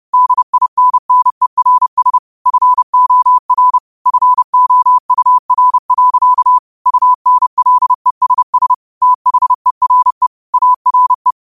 Morse code, heard as a keyed tone, about 30 wpm, 1000 Hz. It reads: DINNERS FOR FOUR. UNLESS THERE ARE